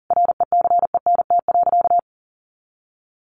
Morse code, heard as a keyed tone, about 37 wpm, 700 Hz. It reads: RECENT.
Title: Sentences from Top 1400 Words 40wpm